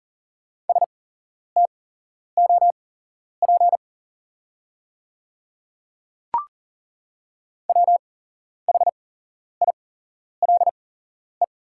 Morse code, heard as a keyed tone, about 39 wpm, spaced as for 11 wpm, 690 Hz. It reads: STOP WHILE
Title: 8x Character Spacing - Top 300 Words - 40wpm